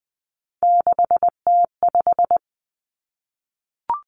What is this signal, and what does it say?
Morse code, heard as a keyed tone, about 20 wpm, 700 Hz. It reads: 6T5
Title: Three Challenging Letter-Number Combinations 20wpm